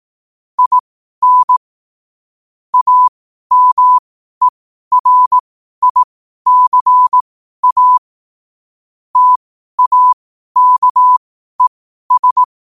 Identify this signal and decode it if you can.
Morse code, heard as a keyed tone, about 18 wpm, spaced as for 12 wpm, 1000 Hz. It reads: IN AMERICA TAKES
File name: qotdfiles / 12wpm